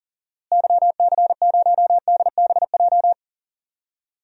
Morse code, heard as a keyed tone, about 40 wpm, 700 Hz. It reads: YC0DBJ
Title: DXCC-100 Call Signs - 6 Characters - Repeated 2x - 40wpm